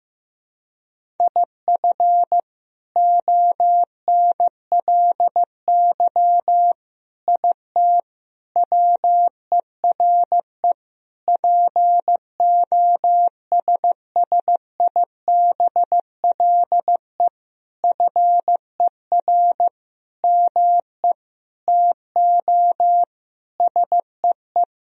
Morse code, heard as a keyed tone, about 15 wpm, 700 Hz. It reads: IF ONLY IT WERE POSSIBLE FER ME TO SEE
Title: Sentences from Top 500 Words and Abbreviations 15wpm